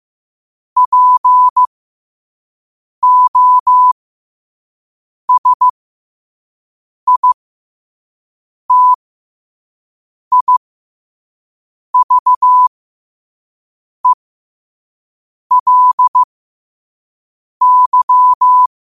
Morse code, heard as a keyed tone, about 15 wpm, spaced as for 5 wpm, 1000 Hz. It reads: POSITIVELY
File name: qotdfiles / 5wpm